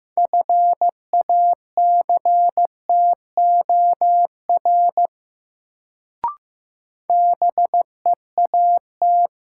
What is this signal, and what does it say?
Morse code, heard as a keyed tone, about 15 wpm, 700 Hz. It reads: FACTOR BEAT
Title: Sentences from Top 1300 Words - New Word Drill Alt - 15wpm